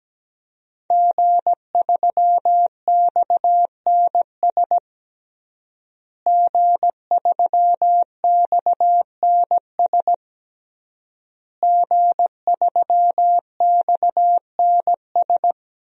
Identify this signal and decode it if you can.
Morse code, heard as a keyed tone, about 17 wpm, 700 Hz. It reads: G3XNS G3XNS G3XNS
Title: DXCC-100 Call Signs - 5 Characters - Repeated 3x - 17wpm